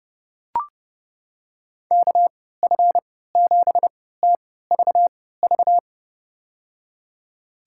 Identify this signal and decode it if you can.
Morse code, heard as a keyed tone, about 31 wpm, spaced as for 17 wpm, 700 Hz. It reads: KF7TVV
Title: USA Call Signs - 3x Character Spacing - 30wpm